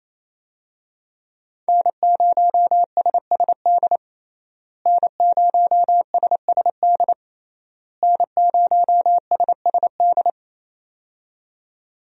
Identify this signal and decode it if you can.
Morse code, heard as a keyed tone, about 27 wpm, 700 Hz. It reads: N0SSD N0SSD N0SSD
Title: USA Call Signs - 5 Characters - Repeated 3x - 28wpm